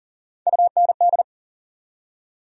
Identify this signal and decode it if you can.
Morse code, heard as a keyed tone, about 40 wpm, 700 Hz. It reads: UND
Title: USA Call Sign Postfixes - 3 Letters 40wpm